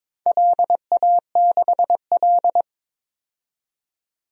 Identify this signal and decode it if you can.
Morse code, heard as a keyed tone, about 22 wpm, 700 Hz. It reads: LA6L
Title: DXCC-100 Call Signs - 4 Characters - 22wpm